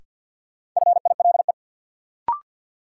Morse code, heard as a keyed tone, about 49 wpm, 700 Hz. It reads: FILE
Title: Sentences from Top 1200 Words - New Word Drill Alt - 50wpm